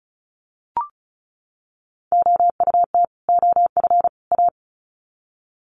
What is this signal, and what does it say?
Morse code, heard as a keyed tone, about 34 wpm, 700 Hz. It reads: OUT OF A
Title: Sets of 3 Words 35wpm